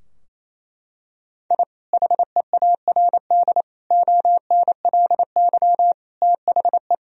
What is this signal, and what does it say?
Morse code, heard as a keyed tone, about 28 wpm, 700 Hz. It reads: I HEARD ONLY THE
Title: Sentences from Top 1000 Words - Review Part 2 - 28wpm